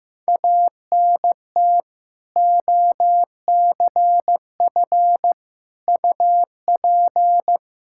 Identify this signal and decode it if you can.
Morse code, heard as a keyed tone, about 15 wpm, 700 Hz. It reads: ANT OCF UP
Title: Practice QSO - Exchange 2 - Info - 15wpm